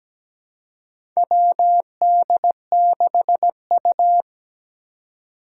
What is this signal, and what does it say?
Morse code, heard as a keyed tone, about 17 wpm, 700 Hz. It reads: WD6U
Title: USA Call Signs - 4 Characters - 4x Character Spacing - 17wpm